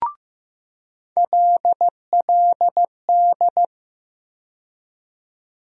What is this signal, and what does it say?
Morse code, heard as a keyed tone, about 15 wpm, 700 Hz. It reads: LLD
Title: USA Call Sign Postfixes - 3 Letters 15wpm